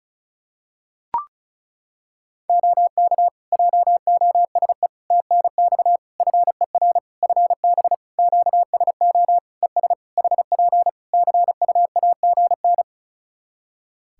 Morse code, heard as a keyed tone, about 34 wpm, 700 Hz. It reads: OK JOSE TNX FER FB QSO ES HP CUAGN
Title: K1USN SST QSO Practice 35wpm